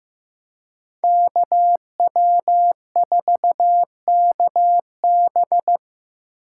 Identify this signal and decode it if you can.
Morse code, heard as a keyed tone, about 18 wpm, 700 Hz. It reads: KW4KB
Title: Speed Racing - USA Call Signs 15wpm